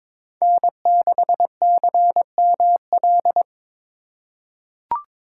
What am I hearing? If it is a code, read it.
Morse code, heard as a keyed tone, about 22 wpm, 700 Hz. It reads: N6CML
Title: USA Call Signs - 5 Characters - 3x Character Spacing - 22wpm